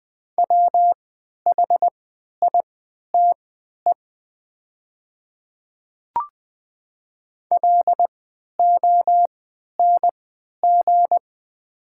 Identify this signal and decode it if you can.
Morse code, heard as a keyed tone, about 20 wpm, spaced as for 11 wpm, 700 Hz. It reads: WHITE LONG
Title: 3x Character Spacing - Top 300 Words - 20wpm